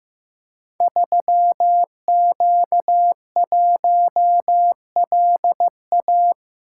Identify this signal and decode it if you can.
Morse code, heard as a keyed tone, about 15 wpm, 700 Hz. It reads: 3Q1LA
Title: Code Groups - 5 Letter:Numbers - 15wpm